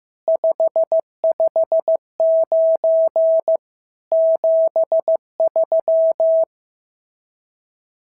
Morse code, heard as a keyed tone, about 15 wpm, 650 Hz. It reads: 559 73
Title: SOTA QSO Practice 15wpm